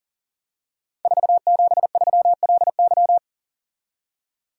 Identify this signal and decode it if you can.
Morse code, heard as a keyed tone, about 42 wpm, 700 Hz. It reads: 473LY
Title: Code Groups - 5 Letter:Numbers - 40wpm